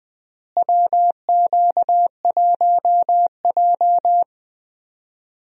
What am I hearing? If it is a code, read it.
Morse code, heard as a keyed tone, about 21 wpm, 700 Hz. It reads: WQ1J
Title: USA Call Signs - 4 Characters - Repeated 3x - 20wpm